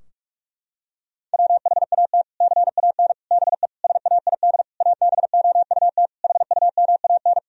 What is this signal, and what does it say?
Morse code, heard as a keyed tone, about 46 wpm, 700 Hz. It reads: WHAT CAN BE SAID ABOUT HUMAN
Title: Sentences from Top 1000 Words - Review Part 4 - 45wpm